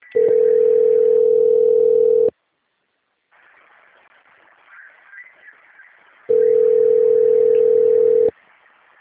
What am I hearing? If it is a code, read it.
Ringback tone (North America), 440 + 480 Hz.
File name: out- - - - - .67490